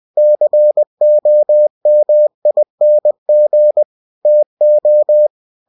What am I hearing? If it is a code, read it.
Morse code, heard as a keyed tone, about 20 wpm, 600 Hz. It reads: COMING TO
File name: cw-books / winnie-the-pooh / winnie-20wpm-page-0081